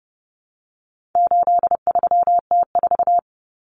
Morse code, heard as a keyed tone, about 30 wpm, 700 Hz. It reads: 83T4
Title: Common Cut Numbers - 4-Digit 30wpm